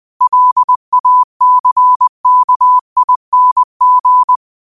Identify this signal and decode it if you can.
Morse code, heard as a keyed tone, about 20 wpm, 1000 Hz. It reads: LACKING
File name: qotdfiles / 20wpm